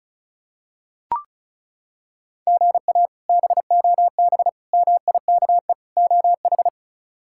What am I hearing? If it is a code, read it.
Morse code, heard as a keyed tone, about 34 wpm, 700 Hz. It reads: GA BOB MIKE OH
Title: K1USN SST QSO Practice - 35wpm